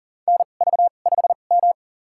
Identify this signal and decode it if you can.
Morse code, heard as a keyed tone, about 40 wpm, spaced as for 29 wpm, 700 Hz. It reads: NV5M
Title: USA Call Signs - 4 Characters - 2x Character Spacing - 40wpm